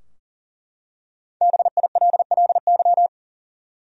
Morse code, heard as a keyed tone, about 40 wpm, 700 Hz. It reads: BILLY
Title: USA Common Names - Max 5 Letters - 40wpm